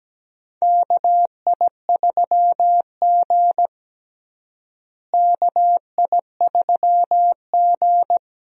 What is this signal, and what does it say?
Morse code, heard as a keyed tone, about 17 wpm, 700 Hz. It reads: KI3G KI3G
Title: USA Call Signs - 4 Characters - Repeated 3x - 17wpm